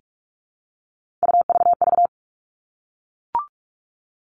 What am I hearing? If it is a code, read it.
Morse code, heard as a keyed tone, about 44 wpm, 700 Hz. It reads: UVV
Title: All Cut Numbers - 3-Digit 45wpm